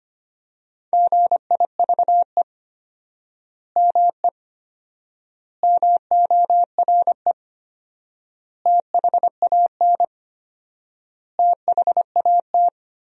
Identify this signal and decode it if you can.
Morse code, heard as a keyed tone, about 25 wpm, 700 Hz. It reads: GIVE ME MORE THAN THAT
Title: 4x Word-Spacing - Sentences from Top 200 Words - 25wpm